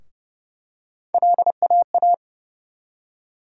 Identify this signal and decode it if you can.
Morse code, heard as a keyed tone, about 30 wpm, 700 Hz. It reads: LAA
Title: USA Call Sign Postfixes - 3 Letters 30wpm